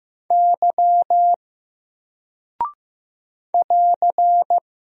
Morse code, heard as a keyed tone, about 15 wpm, 700 Hz. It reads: Y <AR>
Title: Single Character (Letter:Number:Punctuation:Prosign) Alt 15wpm